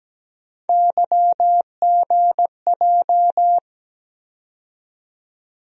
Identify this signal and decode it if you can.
Morse code, heard as a keyed tone, about 17 wpm, 700 Hz. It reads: YGJ
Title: USA Call Sign Postfixes - 3 Letters 17wpm